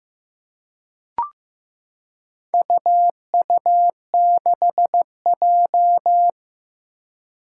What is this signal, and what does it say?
Morse code, heard as a keyed tone, about 15 wpm, 700 Hz. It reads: UU6J
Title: Four Challenging Letter-Number Combinations 15wpm